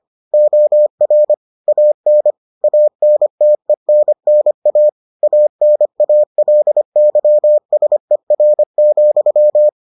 Morse code, heard as a keyed tone, about 25 wpm, 600 Hz. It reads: OR AN ANTENNA ANALYSER,